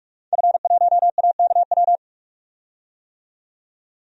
Morse code, heard as a keyed tone, about 44 wpm, 700 Hz. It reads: F1AKW
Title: DXCC-100 Call Signs - 5 Characters - 45wpm